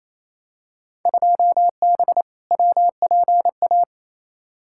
Morse code, heard as a keyed tone, about 28 wpm, 700 Hz. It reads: 2B WPA
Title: Field Day Exchanges Encore - 28wpm